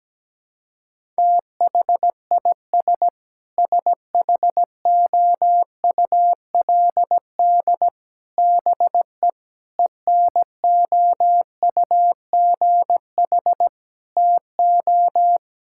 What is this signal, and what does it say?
Morse code, heard as a keyed tone, about 17 wpm, 700 Hz. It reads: THIS SHOULD BE ENOUGH TO
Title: Sentences from Top 1000 Words - Review Part 6 - 17wpm